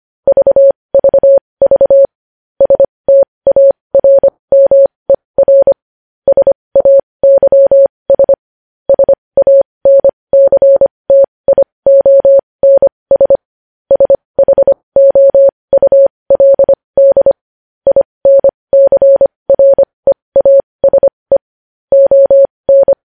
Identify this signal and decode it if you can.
Morse code, heard as a keyed tone, about 25 wpm, spaced as for 20 wpm, 550 Hz. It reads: VVV STARMER SAYS SANCTIONS SHOULD INCREASE ON
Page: news-headlines-in-morse-code-at-20-wpm